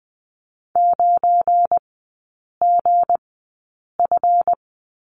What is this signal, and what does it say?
Morse code, heard as a keyed tone, about 20 wpm, spaced as for 8 wpm, 690 Hz. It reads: 9GF
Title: 172 Lesson 28e - Number 9 - Three Characters 20wpm